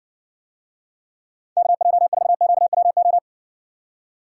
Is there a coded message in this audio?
Yes: Morse code from 1.6 to 3.2 s, heard as a keyed tone, about 60 wpm, 690 Hz. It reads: KJ4QWO